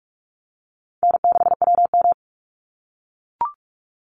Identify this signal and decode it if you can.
Morse code, heard as a keyed tone, about 44 wpm, 700 Hz. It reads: N6WM
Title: Easy Call Signs - 45wpm